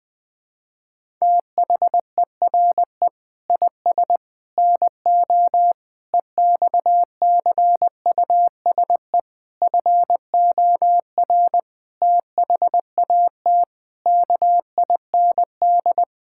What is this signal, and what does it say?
Morse code, heard as a keyed tone, about 20 wpm, 700 Hz. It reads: THERE IS NO EXCUSE FOR THAT KIND
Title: Sentences from Top 1000 Words - Review Part 6 - 20wpm